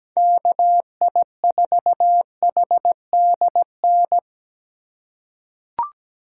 Morse code, heard as a keyed tone, about 17 wpm, 700 Hz. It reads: KI4HDN
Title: USA Call Signs - 3x Character Spacing - 17wpm